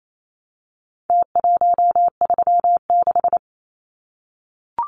Morse code, heard as a keyed tone, about 28 wpm, 700 Hz. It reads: T136
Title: Numbers - Contest Style - 4-Digit 28wpm